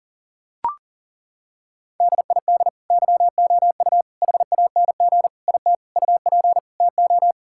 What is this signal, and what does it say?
Morse code, heard as a keyed tone, about 40 wpm, 700 Hz. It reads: DID YOU HANG IT UP TO
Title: Sentences from Top 1600 Words 40wpm